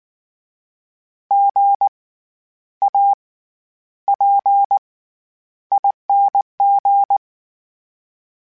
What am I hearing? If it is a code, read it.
Morse code, heard as a keyed tone, about 19 wpm, spaced as for 10 wpm, 800 Hz. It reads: GAPING
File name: Special INGs / Special INGs-mp3s / SlowToFaster ing @18